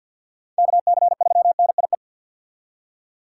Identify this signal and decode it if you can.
Morse code, heard as a keyed tone, about 46 wpm, 690 Hz. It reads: KC3NIE
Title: K1USN SST QSO Practice - 50wpm